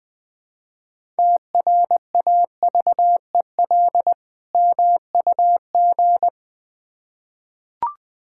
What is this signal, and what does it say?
Morse code, heard as a keyed tone, about 20 wpm, 700 Hz. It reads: TRAVEL MUG
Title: Difficult Words with Hints - 20wpm